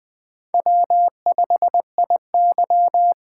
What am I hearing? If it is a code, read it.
Morse code, heard as a keyed tone, about 20 wpm, 700 Hz. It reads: W5IY
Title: USA Call Signs - 4 Characters - 4x Character Spacing - 20wpm